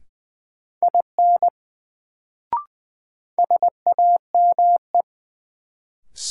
Morse code, heard as a keyed tone, about 20 wpm, 700 Hz. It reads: IN SAME
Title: Sets of 2 Words - New Word Drill - 20wpm